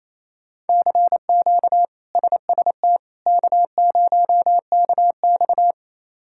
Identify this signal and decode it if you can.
Morse code, heard as a keyed tone, about 27 wpm, 700 Hz. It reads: CQ SST K0KX
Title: K1USN SST QSO Practice - 28wpm